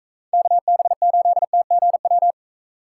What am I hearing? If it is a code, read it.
Morse code, heard as a keyed tone, about 35 wpm, 700 Hz. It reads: KB8TGW
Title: Speed Racing - USA 2x3 Call Signs 28wpm